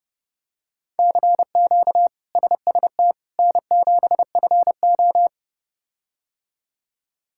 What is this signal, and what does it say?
Morse code, heard as a keyed tone, about 29 wpm, 700 Hz. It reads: CQ SST N7FO
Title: K1USN SST QSO Practice - 30wpm